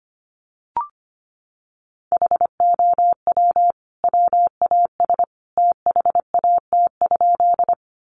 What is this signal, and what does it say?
Morse code, heard as a keyed tone, about 25 wpm, 700 Hz. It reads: HOW WAS THAT?
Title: Sentences from Top 100 Words 25wpm